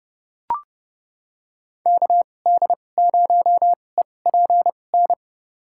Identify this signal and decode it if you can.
Morse code, heard as a keyed tone, about 30 wpm, spaced as for 22 wpm, 700 Hz. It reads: KD0EPN
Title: USA Call Signs - 6 Characters - 2x Character Spacing - 30wpm